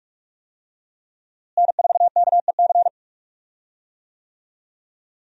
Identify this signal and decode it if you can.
Morse code, heard as a keyed tone, about 46 wpm, 700 Hz. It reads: N4KEC
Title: USA Call Signs - 5 Characters - Repeated 3x - 45wpm